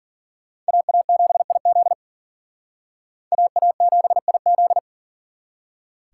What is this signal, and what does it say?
Morse code, heard as a keyed tone, about 51 wpm, 700 Hz. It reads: AA7IZ AA7IZ